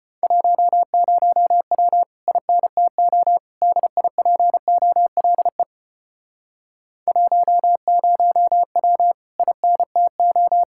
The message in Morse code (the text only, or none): 10W INTO DIPOLE 10W INTO